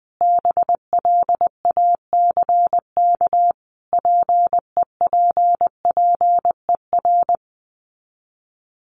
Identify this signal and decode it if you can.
Morse code, heard as a keyed tone, about 20 wpm, 700 Hz. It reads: BLACK PEPPER
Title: Sets of 5 Words Encore 20wpm